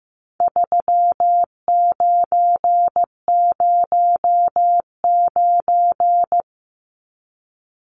Morse code, heard as a keyed tone, about 15 wpm, 700 Hz. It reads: 3909
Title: Numbers - 4-Digit 15wpm